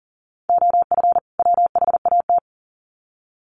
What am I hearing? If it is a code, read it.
Morse code, heard as a keyed tone, about 40 wpm, 700 Hz. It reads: OF WHAT
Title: Sets of 2 Words 40wpm